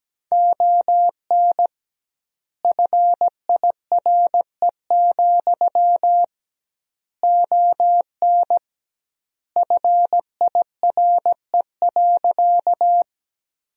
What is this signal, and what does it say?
Morse code, heard as a keyed tone, about 17 wpm, 700 Hz. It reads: ON FIRE, ON FIRE.
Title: 2x Word-Spacing - Sentences from Top 400 Words - 17wpm